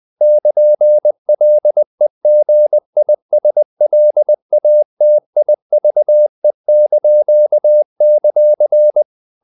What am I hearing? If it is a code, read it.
Morse code, heard as a keyed tone, about 20 wpm, 600 Hz. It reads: <KN>LEGISLATIVE);
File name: CW / Practice / NationalCapitals20